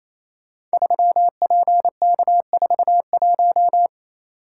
Morse code, heard as a keyed tone, about 27 wpm, 700 Hz. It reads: 3PK41
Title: Code Groups - 5 Letter:Numbers - 28wpm